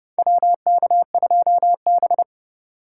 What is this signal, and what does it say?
Morse code, heard as a keyed tone, about 32 wpm, 700 Hz. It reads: WK2B
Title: USA Call Signs - 4 Characters - 3x Character Spacing - 30wpm